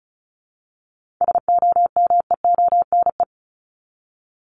Morse code, heard as a keyed tone, about 34 wpm, 700 Hz. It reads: SOMEONE
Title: Top 501-750 Words 35wpm